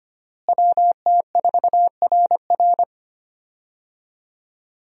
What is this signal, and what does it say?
Morse code, heard as a keyed tone, about 25 wpm, 700 Hz. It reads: WT4RR
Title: USA Call Signs - 5 Characters - Repeated 2x - 25wpm